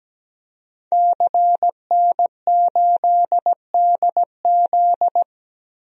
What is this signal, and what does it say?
Morse code, heard as a keyed tone, about 17 wpm, 700 Hz. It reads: CN8DZ
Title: DXCC-100 Call Signs - 5 Characters - 17wpm